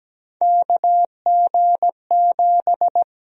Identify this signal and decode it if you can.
Morse code, heard as a keyed tone, about 17 wpm, 700 Hz. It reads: KG7
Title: USA Call Sign Prefixes - 17wpm